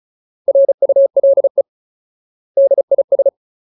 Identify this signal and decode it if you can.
Morse code, heard as a keyed tone, about 34 wpm, 550 Hz. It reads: RULE DIS
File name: Ultimate Word List / 35WPM Ultimate